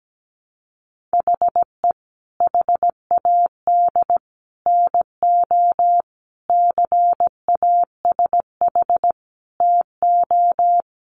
Morse code, heard as a keyed tone, about 17 wpm, 700 Hz. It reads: HE HAD NO CASH TO
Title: Sentences from Top 1100 Words 17wpm